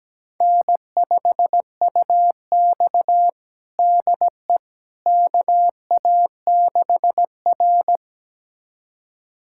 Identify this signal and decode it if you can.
Morse code, heard as a keyed tone, about 17 wpm, 700 Hz. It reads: N5UX DE KA6R
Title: USA Callsign DE Callsign - 4 Characters - 17wpm